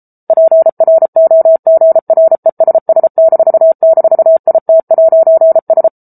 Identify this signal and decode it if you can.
Morse code, heard as a keyed tone, about 34 wpm, 650 Hz. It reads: PROGRESS--IT'S